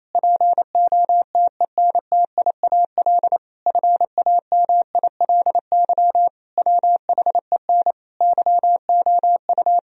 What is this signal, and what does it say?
Morse code, heard as a keyed tone, about 28 wpm, 700 Hz. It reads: POTENTIAL FAMILY WHEN YOU
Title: Sentences from Top 1000 Words - Review Part 4 - 28wpm